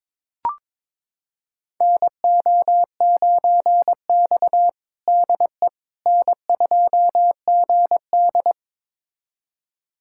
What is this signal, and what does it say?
Morse code, heard as a keyed tone, about 22 wpm, 700 Hz. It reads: NO9X DE N2GD
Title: USA Callsign DE Callsign - 4 Characters - 22wpm